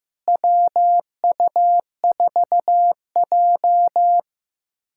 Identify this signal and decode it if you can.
Morse code, heard as a keyed tone, about 15 wpm, 700 Hz. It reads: WU4J
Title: USA Callsign DE Callsign - 4 Characters - 15wpm